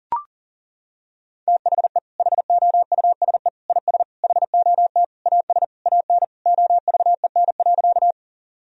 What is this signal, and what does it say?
Morse code, heard as a keyed tone, about 40 wpm, 700 Hz. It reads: THE HOUSE IS HOT AS AN OVEN.
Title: Sentences from Top 1000 Words - Review Part 4 - 40wpm